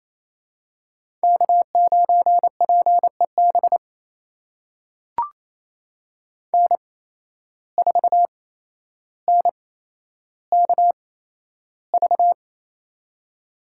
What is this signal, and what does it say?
Morse code, heard as a keyed tone, about 28 wpm, spaced as for 12 wpm, 700 Hz. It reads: K9PEB N4NKV
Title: USA Call Signs - 5 Characters - 8x Character Spacing - 28wpm